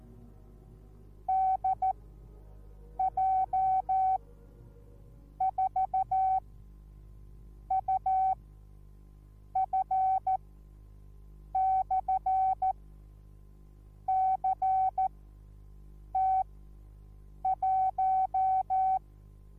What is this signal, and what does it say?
Morse code, heard as a keyed tone, about 14 wpm, spaced as for 6 wpm, 760 Hz. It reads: DJ4UF/CT1